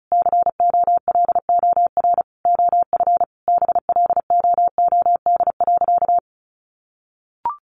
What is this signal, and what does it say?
Morse code, heard as a keyed tone, about 34 wpm, 700 Hz. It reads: COLOR OF BLOOD.
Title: A Sound Of Thunder 35wpm